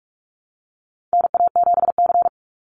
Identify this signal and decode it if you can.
Morse code, heard as a keyed tone, about 46 wpm, 700 Hz. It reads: NA7C
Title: Call Signs - 45wpm